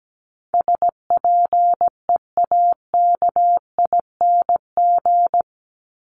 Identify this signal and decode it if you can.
Morse code, heard as a keyed tone, about 17 wpm, 700 Hz. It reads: SPEAKING
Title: ING Suffix - Top 100 Words with Base Words 17wpm